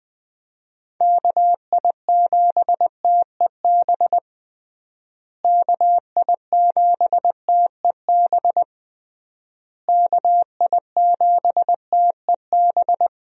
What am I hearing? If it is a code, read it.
Morse code, heard as a keyed tone, about 20 wpm, 700 Hz. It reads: KI7TEB KI7TEB KI7TEB
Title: USA Call Signs - 6 Characters - Repeated 3x - 20wpm